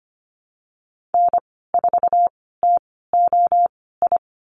Morse code, heard as a keyed tone, about 25 wpm, spaced as for 16 wpm, 700 Hz. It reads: N4TOI
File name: Lesson 15g - Number 5 - Call Signs 25wpm